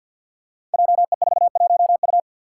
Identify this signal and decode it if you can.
Morse code, heard as a keyed tone, about 51 wpm, 700 Hz. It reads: JE41U
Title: Five Challenging Letter-Number Combinations 50wpm